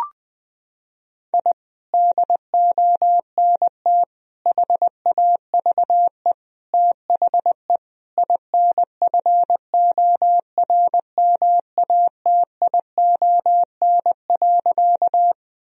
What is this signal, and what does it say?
Morse code, heard as a keyed tone, about 20 wpm, 700 Hz. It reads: I DONT HAVE THE INFORMATION.
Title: Sentences from Top 1000 Words - Review Part 3 - 20wpm